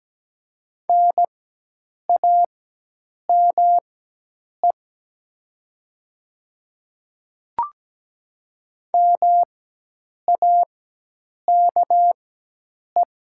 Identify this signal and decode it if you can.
Morse code, heard as a keyed tone, about 17 wpm, spaced as for 8 wpm, 700 Hz. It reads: NAME MAKE
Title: 4x Character Spacing - Top 300 Words - 17wpm